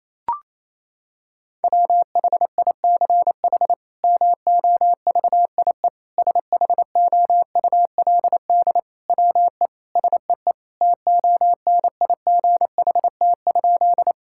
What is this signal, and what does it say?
Morse code, heard as a keyed tone, about 28 wpm, 700 Hz. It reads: WHICH MOVIE SHOULD WE SEE TONIGHT?
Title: Sentences from Top 1000 Words - Review Part 5 - 28wpm